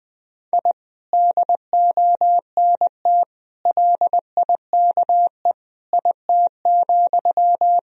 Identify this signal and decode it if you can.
Morse code, heard as a keyed tone, about 20 wpm, 700 Hz. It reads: I DONT LIKE IT,
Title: Sentences from Top 1000 Words - Review Part 1 - 20wpm